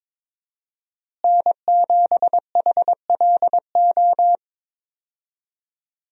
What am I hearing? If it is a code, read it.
Morse code, heard as a keyed tone, about 22 wpm, 700 Hz. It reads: N7HLO
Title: USA Call Signs - 5 Characters - Repeated 2x - 22wpm